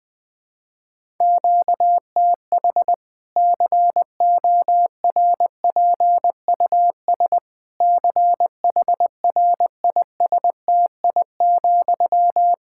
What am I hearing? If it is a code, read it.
Morse code, heard as a keyed tone, about 21 wpm, 700 Hz. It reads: QTH CORPUS CHRISTI,
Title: Practice QSO - Exchange 1 - Beginning - 20wpm